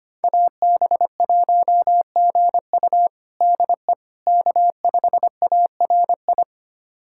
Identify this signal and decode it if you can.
Morse code, heard as a keyed tone, about 25 wpm, 700 Hz. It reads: AB1GU DE K5ARI
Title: USA Callsign DE Callsign - 5 Characters - 25wpm